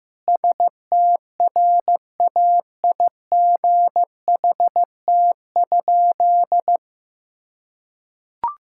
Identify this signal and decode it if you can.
Morse code, heard as a keyed tone, about 15 wpm, 700 Hz. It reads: STRAIGHT?
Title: Sentences from Top 1000 Words - Review Part 6 - 15wpm